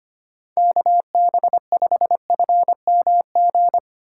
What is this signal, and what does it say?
Morse code, heard as a keyed tone, about 25 wpm, 700 Hz. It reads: KB5FMG
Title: USA Call Signs - 6 Characters - 2x Character Spacing - 25wpm